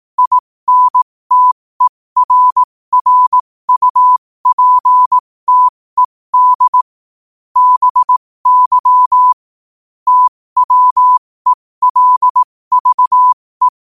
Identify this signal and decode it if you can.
Morse code, heard as a keyed tone, about 18 wpm, spaced as for 15 wpm, 1000 Hz. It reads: INTERRUPTED BY TWELVE
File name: qotdfiles / 15wpm